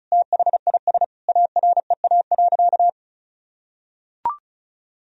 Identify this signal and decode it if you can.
Morse code, heard as a keyed tone, about 34 wpm, 700 Hz. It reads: THIS AREA.
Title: Sentences from Top 1000 Words - Review Part 2 - 35wpm